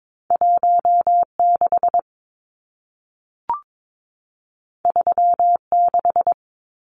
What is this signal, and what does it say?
Morse code, heard as a keyed tone, about 22 wpm, 700 Hz. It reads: 16 36
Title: Numbers - 2-Digit 22wpm